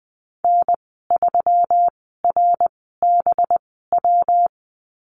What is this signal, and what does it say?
Morse code, heard as a keyed tone, about 20 wpm, spaced as for 14 wpm, 700 Hz. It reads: N3RBW
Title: 200 Lesson 32g - Letter k - Call Signs 20wpm